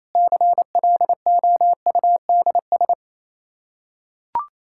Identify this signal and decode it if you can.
Morse code, heard as a keyed tone, about 28 wpm, 700 Hz. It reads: CLOUDS
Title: Practice QSO - Exchange 2 - Info - 28wpm